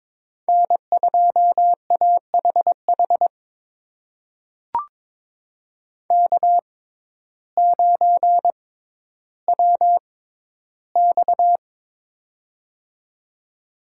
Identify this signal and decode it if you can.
Morse code, heard as a keyed tone, about 22 wpm, spaced as for 12 wpm, 700 Hz. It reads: N2AHH K9WX
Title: USA Call Signs - 6x Character Spacing - 22wpm